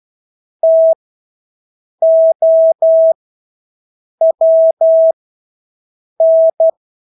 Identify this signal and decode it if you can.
Morse code, heard as a keyed tone, about 12 wpm, spaced as for 6 wpm, 650 Hz. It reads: TOWN
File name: Engelske ord 12-6 wpm